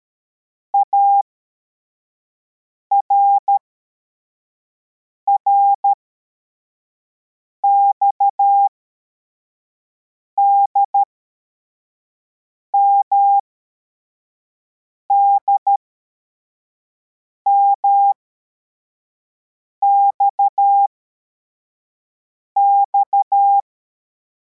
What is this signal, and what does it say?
Morse code, heard as a keyed tone, about 13 wpm, spaced as for 4 wpm, 800 Hz. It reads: ARRXD MDMXX
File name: formation / cw / Text21